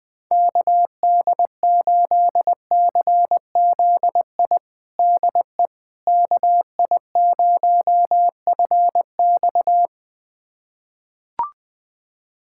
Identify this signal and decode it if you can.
Morse code, heard as a keyed tone, about 20 wpm, 700 Hz. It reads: KD8CZI DE KI0FX
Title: USA Callsign DE Callsign 20wpm